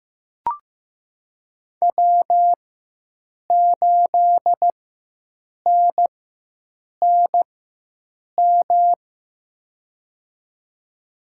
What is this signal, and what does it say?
Morse code, heard as a keyed tone, about 15 wpm, spaced as for 7 wpm, 700 Hz. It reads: W8NNM
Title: USA Call Signs - 5 Characters - 4x Character Spacing - 15wpm